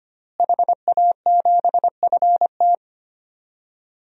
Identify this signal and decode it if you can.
Morse code, heard as a keyed tone, about 25 wpm, 700 Hz. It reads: HA7FT
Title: DXCC-100 Call Signs - 5 Characters - Repeated 3x - 25wpm